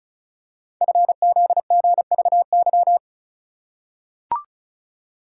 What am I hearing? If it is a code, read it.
Morse code, heard as a keyed tone, about 36 wpm, 700 Hz. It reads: FZGVY
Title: Code Groups - 5 Letters - 35wpm